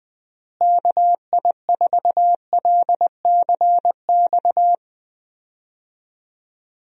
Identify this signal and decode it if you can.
Morse code, heard as a keyed tone, about 20 wpm, 700 Hz. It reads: KI4LCX
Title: USA Call Signs - 6 Characters - Repeated 2x - 20wpm